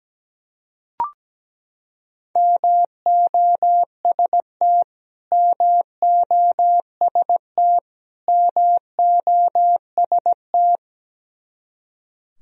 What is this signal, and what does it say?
Morse code, heard as a keyed tone, about 17 wpm, 700 Hz. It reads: MOST MOST MOST
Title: Top 100 Words Repeated 17wpm